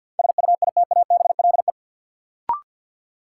Morse code, heard as a keyed tone, about 53 wpm, 700 Hz. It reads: SUITABLE
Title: Sentences from Top 1400 Words - New Word Drill Alt - 50wpm